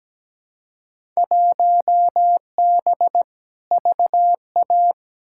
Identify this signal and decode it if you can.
Morse code, heard as a keyed tone, about 17 wpm, 700 Hz. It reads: 1B VA
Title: Field Day Exchanges Encore - 17wpm